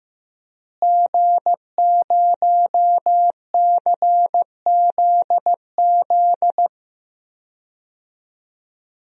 Morse code, heard as a keyed tone, about 15 wpm, 700 Hz. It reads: G0CZZ
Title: DXCC-100 Call Signs - 5 Characters - Repeated 3x - 15wpm